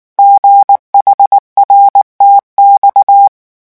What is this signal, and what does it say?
Morse code, heard as a keyed tone, about 19 wpm, 800 Hz. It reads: GHRTX
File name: morse mp3 / revision7-vitesse 18 mots